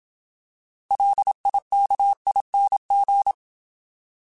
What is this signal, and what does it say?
Morse code, heard as a keyed tone, about 27 wpm, 800 Hz. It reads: LIKING